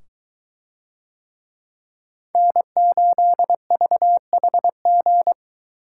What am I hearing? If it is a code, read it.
Morse code, heard as a keyed tone, about 22 wpm, 700 Hz. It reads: N8VHG